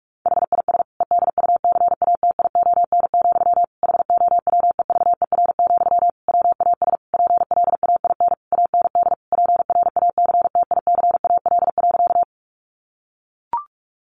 Morse code, heard as a keyed tone, about 44 wpm, 700 Hz. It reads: HIS EDUCATION, HOWEVER, WAS PLAIN AND PRACTICAL.
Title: Sentences from Top 1400 Words 45wpm